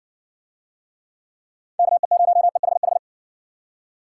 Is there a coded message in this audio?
Yes: Morse code from 1.8 to 3.0 s, heard as a keyed tone, about 57 wpm, 700 Hz. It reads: KE0EHH